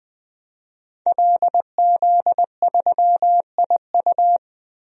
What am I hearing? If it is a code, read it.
Morse code, heard as a keyed tone, about 20 wpm, 700 Hz. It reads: LZ3IU
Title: DXCC-100 Call Signs - 5 Characters - 20wpm